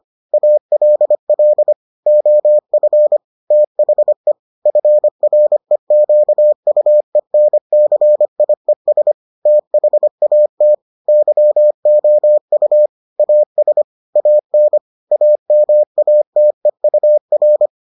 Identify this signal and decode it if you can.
Morse code, heard as a keyed tone, about 25 wpm, 600 Hz. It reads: ALL OF THE FREQUENCIES THAT YOU AS AN AMATEUR